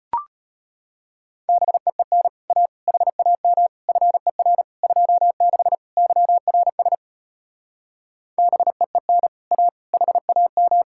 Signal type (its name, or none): Morse code